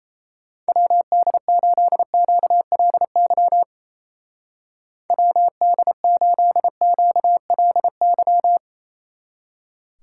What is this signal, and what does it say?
Morse code, heard as a keyed tone, about 35 wpm, 700 Hz. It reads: WD8QLY WD8QLY